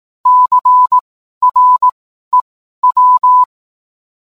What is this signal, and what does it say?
Morse code, heard as a keyed tone, about 18 wpm, spaced as for 12 wpm, 1000 Hz. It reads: CREW